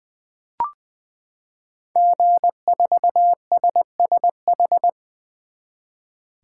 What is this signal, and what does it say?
Morse code, heard as a keyed tone, about 20 wpm, 700 Hz. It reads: G4SSH
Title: 2023 SOTA Chaser Call Signs - 20wpm